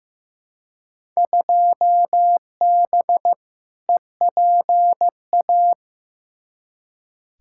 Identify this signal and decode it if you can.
Morse code, heard as a keyed tone, about 15 wpm, 700 Hz. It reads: 2B EPA
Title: Field Day Exchanges Encore - 15wpm